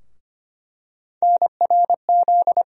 Morse code, heard as a keyed tone, about 25 wpm, 700 Hz. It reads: NRZ
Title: USA Call Sign Postfixes - 3 Letters 25wpm